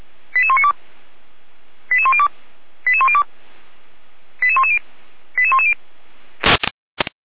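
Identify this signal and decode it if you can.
Five-tone selective call (ZVEI), heard as quick runs of five tones, tones of about 70 ms. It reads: 88192 88102 88102 88110 88110
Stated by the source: ZVEI five-tone selective call 88109, received by SDR.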